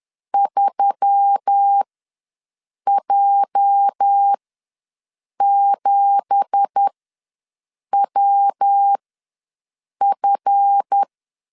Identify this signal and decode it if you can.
Morse code, heard as a keyed tone, about 10 wpm, spaced as for 6 wpm, 800 Hz. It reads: 3J7WF
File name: LEZIONI / 10 SIGNS / 10D COMA ws 10wpm 4314